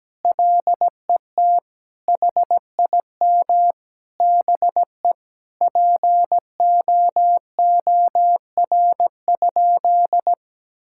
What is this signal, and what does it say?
Morse code, heard as a keyed tone, about 17 wpm, 700 Hz. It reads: LET HIM BE POOR?
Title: Sentences from Top 1600 Words 17wpm